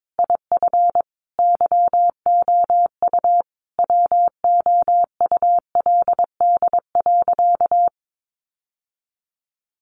Morse code, heard as a keyed tone, about 22 wpm, 700 Hz. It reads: IF YOU WOULD.
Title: Sentences from Top 1500 Words 22wpm